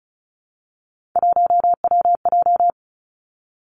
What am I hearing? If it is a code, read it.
Morse code, heard as a keyed tone, about 36 wpm, 700 Hz. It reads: 1WJ
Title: Three Challenging Letter-Number Combinations 35wpm